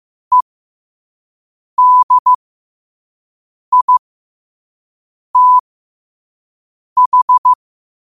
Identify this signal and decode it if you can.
Morse code, heard as a keyed tone, about 15 wpm, spaced as for 5 wpm, 1000 Hz. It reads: EDITH
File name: qotdfiles / 5wpm